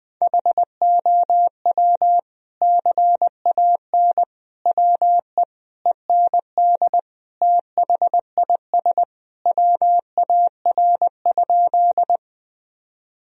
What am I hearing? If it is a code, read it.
Morse code, heard as a keyed tone, about 21 wpm, 700 Hz. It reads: HOW CAN WE END THIS WAR?
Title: Sentences from Top 1000 Words - Review Part 2 - 20wpm